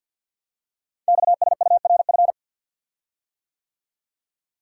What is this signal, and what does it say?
Morse code, heard as a keyed tone, about 50 wpm, 700 Hz. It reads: XSURF
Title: Code Groups - 5 Letters - 50wpm